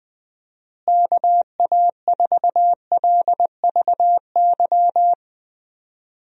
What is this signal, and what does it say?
Morse code, heard as a keyed tone, about 21 wpm, 700 Hz. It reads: KA4LVY
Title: USA Call Signs - 6 Characters - Repeated 2x - 20wpm